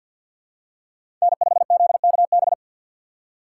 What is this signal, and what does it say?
Morse code, heard as a keyed tone, about 52 wpm, 700 Hz. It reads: N5ZKB
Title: USA Call Signs - 5 Characters - 6x Character Spacing - 50wpm